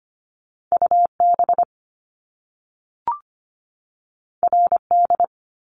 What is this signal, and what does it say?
Morse code, heard as a keyed tone, about 25 wpm, 700 Hz. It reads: UB RD
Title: Top Two Letter Combinations in Words 25wpm